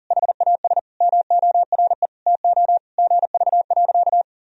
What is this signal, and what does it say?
Morse code, heard as a keyed tone, about 40 wpm, 700 Hz. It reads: HAS MORE TO GV.
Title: Sentences from Top 200 Words and Abbreviations 40wpm